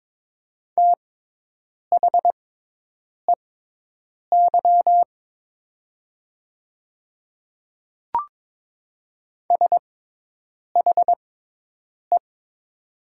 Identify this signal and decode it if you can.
Morse code, heard as a keyed tone, about 22 wpm, spaced as for 8 wpm, 700 Hz. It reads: THEY SHE
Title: 6x Character Spacing - Top 500 Words - 22wpm